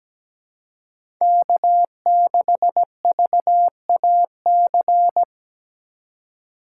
Morse code, heard as a keyed tone, about 17 wpm, 700 Hz. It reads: K6VAC
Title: USA Call Signs - 2x Character Spacing - 17wpm